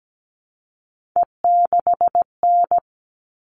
Morse code, heard as a keyed tone, about 17 wpm, 700 Hz. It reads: E6N
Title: All Cut Numbers - 3-Digit 17wpm